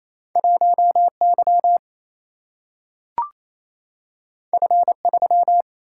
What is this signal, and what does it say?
Morse code, heard as a keyed tone, about 28 wpm, 700 Hz. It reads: 1 Y F 3
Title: Single-Letter Single-Number 28wpm